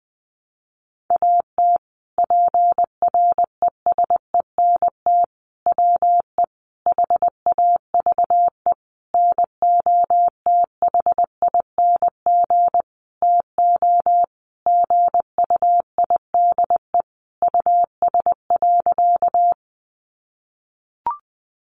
Morse code, heard as a keyed tone, about 20 wpm, 700 Hz. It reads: AT PRESENT WE HAVE NOTHING TO GUIDE US.
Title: Sentences from Top 800 Words 20wpm